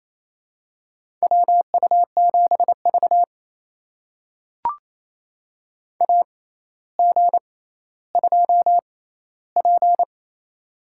Morse code, heard as a keyed tone, about 28 wpm, spaced as for 14 wpm, 700 Hz. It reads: WU7V AG2P
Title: USA Call Signs - 4 Characters - 6x Character Spacing - 28wpm